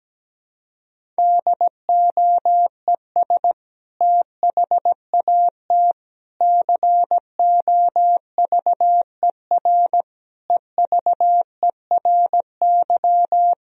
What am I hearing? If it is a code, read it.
Morse code, heard as a keyed tone, about 17 wpm, 700 Hz. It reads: DOES THAT COVER EVERY
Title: Sentences from Top 1000 Words - Review Part 1 - 17wpm